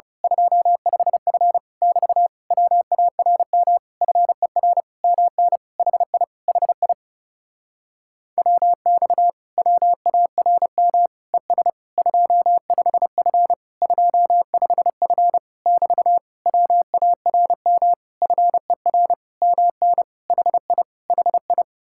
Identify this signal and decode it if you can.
Morse code, heard as a keyed tone, about 30 wpm, 700 Hz. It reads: 25F <BT> WARM FER MN HI HI WX WARM ES 25F 25F <BT> WARM FER MN HI HI
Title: Speed-Racing Common QSO Phrases Encore - 30wpm